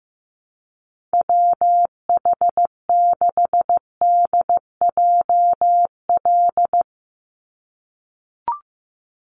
Morse code, heard as a keyed tone, about 15 wpm, 700 Hz. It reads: WH6DJL
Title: USA Call Signs 15wpm